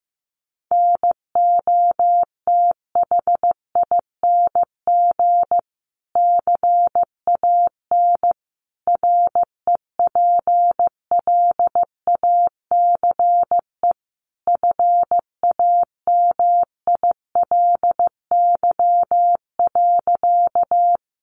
Morse code, heard as a keyed tone, about 15 wpm, 700 Hz. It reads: NOTHING CAN REPLACE FAMILY.
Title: Sentences from Top 1400 Words 15wpm